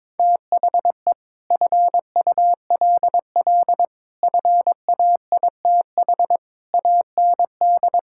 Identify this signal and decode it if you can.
Morse code, heard as a keyed tone, about 21 wpm, 700 Hz. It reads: THE FULL FAITH AND
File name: practice / 22wpm / Sets of 8 Words - 22wpm